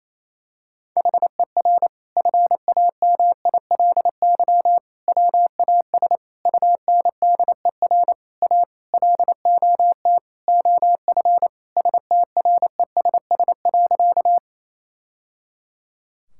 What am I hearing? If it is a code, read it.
Morse code, heard as a keyed tone, about 28 wpm, 700 Hz. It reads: HER FAMILY WAS UNDER A LOT OF STRESS.
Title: Sentences from Top 1000 Words - Review Part 6 - 28wpm